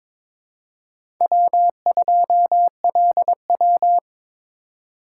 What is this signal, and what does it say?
Morse code, heard as a keyed tone, about 22 wpm, 700 Hz. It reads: W2LW
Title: USA Call Signs - 4 Characters - 2x Character Spacing - 22wpm